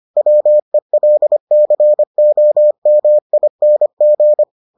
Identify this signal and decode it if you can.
Morse code, heard as a keyed tone, about 25 wpm, 600 Hz. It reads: WELCOMING